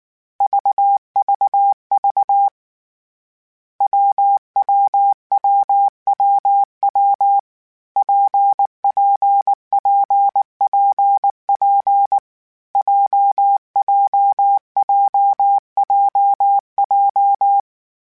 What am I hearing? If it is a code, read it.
Morse code, heard as a keyed tone, about 19 wpm, 800 Hz. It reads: VVV WWWWW PPPPP JJJJJ
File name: Difficult Combos / Difficult Combos-mp3s / WPJ WPJ1 @18